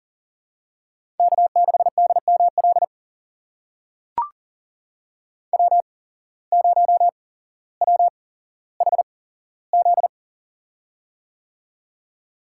Morse code, heard as a keyed tone, about 37 wpm, spaced as for 17 wpm, 700 Hz. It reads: K6DML W0WHZ
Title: USA Call Signs - 8x Character Spacing - 40wpm